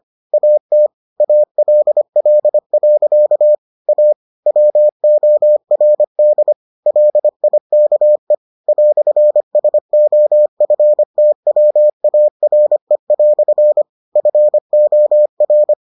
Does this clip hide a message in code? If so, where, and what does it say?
Yes: Morse code from 0.3 to 15.7 s, heard as a keyed tone, about 25 wpm, 600 Hz. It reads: AT ALL. A WORD LIKE "SOFTWARE" FOR